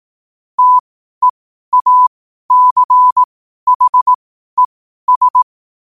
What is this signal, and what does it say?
Morse code, heard as a keyed tone, about 18 wpm, spaced as for 12 wpm, 1000 Hz. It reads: TEACHES
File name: qotdfiles / 12wpm